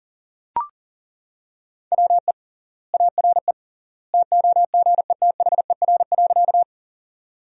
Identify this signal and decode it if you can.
Morse code, heard as a keyed tone, about 37 wpm, 700 Hz. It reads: WE ARE TOGETHER.
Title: 3x Word-Spacing - Sentences from Top 300 Words - 40wpm